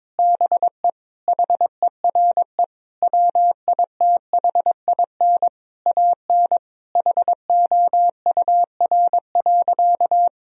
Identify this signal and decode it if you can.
Morse code, heard as a keyed tone, about 21 wpm, 700 Hz. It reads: BE HERE WITHIN AN HOUR.
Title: Sentences from Top 1000 Words - Review Part 4 - 22wpm